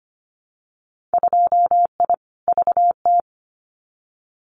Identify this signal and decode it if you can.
Morse code, heard as a keyed tone, about 25 wpm, 700 Hz. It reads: 2I VT
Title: Winter Field Day Exchanges 25wpm